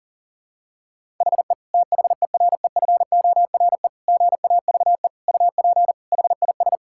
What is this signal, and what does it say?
Morse code, heard as a keyed tone, about 37 wpm, 700 Hz. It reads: HE THEREFORE GAVE UP HIS
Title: Sentences from Top 1000 Words - Review Part 5 - 40wpm